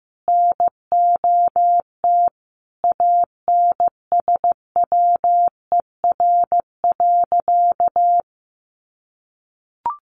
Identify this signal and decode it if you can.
Morse code, heard as a keyed tone, about 15 wpm, 700 Hz. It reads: NOT ANSWER.
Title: Sentences from Top 600 Words 15wpm